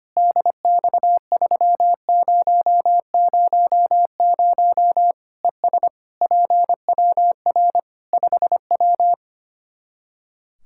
Morse code, heard as a keyed tone, about 25 wpm, 700 Hz. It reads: DX3000 ES PWR 5W
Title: Practice QSO - Exchange 2 - Info - 25wpm